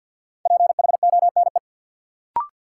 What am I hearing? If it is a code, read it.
Morse code, heard as a keyed tone, about 53 wpm, 700 Hz. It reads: PHONE
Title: Sentences from Top 1100 Words - New Word Drill - 50wpm